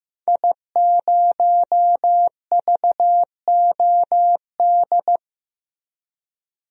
Morse code, heard as a keyed tone, about 15 wpm, 700 Hz. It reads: I0VOD
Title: DXCC-100 Call Signs - 5 Characters - 15wpm